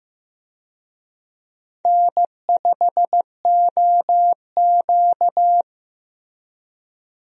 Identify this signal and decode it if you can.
Morse code, heard as a keyed tone, about 15 wpm, 700 Hz. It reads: N5OQ
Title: USA Call Signs - 4 Characters - Repeated 2x - 15wpm